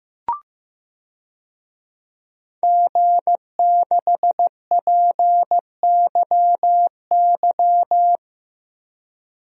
Morse code, heard as a keyed tone, about 15 wpm, 700 Hz. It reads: G6PYY
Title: DXCC-100 Call Signs - 5 Characters - Repeated 3x - 15wpm